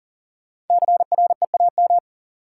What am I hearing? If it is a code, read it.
Morse code, heard as a keyed tone, about 40 wpm, 700 Hz. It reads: CREAM
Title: Sentences from Top 1500 Words - New Word Drill - 40wpm